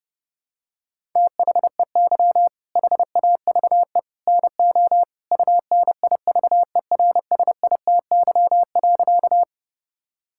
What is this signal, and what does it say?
Morse code, heard as a keyed tone, about 30 wpm, 700 Hz. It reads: THEY HAVE NO UNIVERSITY.
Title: Sentences from Top 1000 Words - Review Part 5 - 30wpm